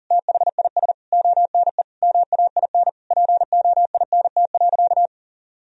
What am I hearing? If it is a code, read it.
Morse code, heard as a keyed tone, about 40 wpm, 700 Hz. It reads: THIS ONE MAIN POINT.
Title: Sentences from Top 1000 Words - Review Part 2 - 40wpm